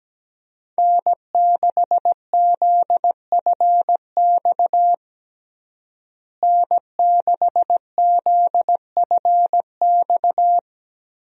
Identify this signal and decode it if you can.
Morse code, heard as a keyed tone, about 17 wpm, 700 Hz. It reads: N6ZFX N6ZFX
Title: USA Call Signs - 5 Characters - Repeated 3x - 17wpm